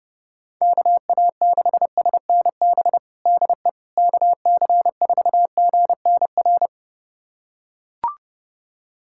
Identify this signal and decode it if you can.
Morse code, heard as a keyed tone, about 30 wpm, 700 Hz. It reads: KA6SNB DE KC4GNR
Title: USA Callsign DE Callsign - 6 Characters - 30wpm